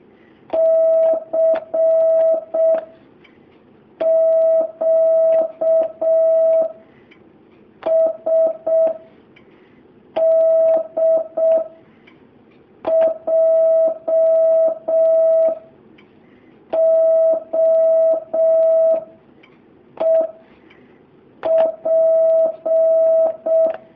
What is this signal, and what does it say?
Morse code, heard as a keyed tone, about 6 wpm, 650 Hz. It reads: C Q S D J O E P
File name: jh8cbh / 2019cw